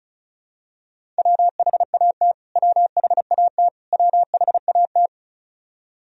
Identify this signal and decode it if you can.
Morse code, heard as a keyed tone, about 36 wpm, 700 Hz. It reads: WHAT WHAT WHAT
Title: Top 100 Words Repeated 35wpm